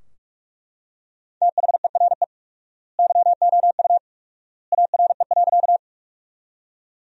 Morse code, heard as a keyed tone, about 41 wpm, 700 Hz. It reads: THERE YOU ARE.
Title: 4x Word-Spacing - Sentences from Top 100 Words - 45wpm